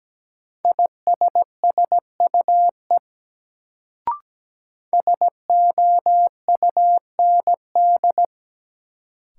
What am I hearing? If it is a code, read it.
Morse code, heard as a keyed tone, about 17 wpm, 700 Hz. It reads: ISSUE SOUND
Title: Sentences from Top 900 Words - New Word Drill - 17wpm